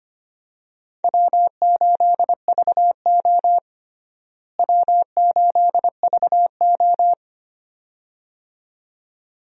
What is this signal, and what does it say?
Morse code, heard as a keyed tone, about 25 wpm, 700 Hz. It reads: W8VO W8VO
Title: USA Call Signs - 4 Characters - Repeated 2x - 25wpm